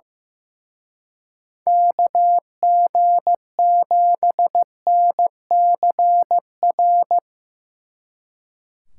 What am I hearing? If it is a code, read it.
Morse code, heard as a keyed tone, about 16 wpm, 700 Hz. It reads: KG7NCR
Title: Speed Racing - USA 2x3 Call Signs 15wpm